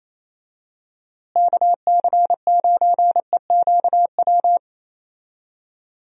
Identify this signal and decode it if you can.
Morse code, heard as a keyed tone, about 27 wpm, 700 Hz. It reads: KC9EQW
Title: USA Call Signs - 6 Characters - Repeated 2x - 28wpm